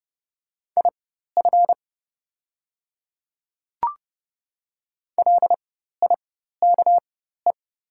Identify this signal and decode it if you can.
Morse code, heard as a keyed tone, about 30 wpm, spaced as for 14 wpm, 690 Hz. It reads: IF LIKE
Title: 4x Character Spacing - Top 100 Words - 30wpm